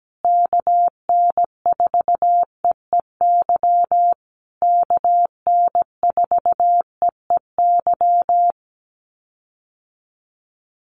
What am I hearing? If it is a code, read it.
Morse code, heard as a keyed tone, about 17 wpm, 700 Hz. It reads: KN4EEY KN4EEY
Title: USA Call Signs Repeated 3x 17wpm